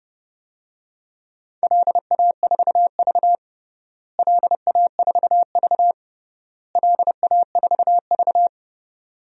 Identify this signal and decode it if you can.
Morse code, heard as a keyed tone, about 30 wpm, 700 Hz. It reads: LA4V LA4V LA4V
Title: DXCC-100 Call Signs - 4 Characters - Repeated 3x - 30wpm